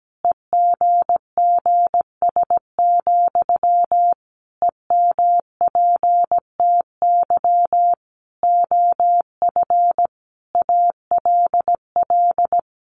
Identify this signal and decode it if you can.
Morse code, heard as a keyed tone, about 17 wpm, 700 Hz. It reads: EGGS, EMPTY OF ALL
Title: A Sound Of Thunder 17wpm